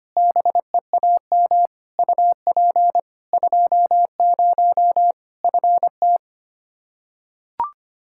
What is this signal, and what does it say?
Morse code, heard as a keyed tone, about 25 wpm, 700 Hz. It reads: BEAM UP 20 FT
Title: Practice QSO - Exchange 2 - Info - 25wpm